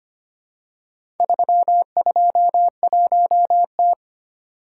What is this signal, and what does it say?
Morse code, heard as a keyed tone, about 25 wpm, 700 Hz. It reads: 321T
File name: practice / 25wpm / Four Letter-Number Combinations - 25wpm